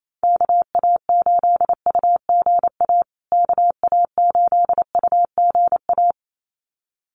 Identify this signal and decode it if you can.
Morse code, heard as a keyed tone, about 28 wpm, 700 Hz. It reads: KA8UGA KA8UGA
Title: USA Call Signs Repeated 2x 28wpm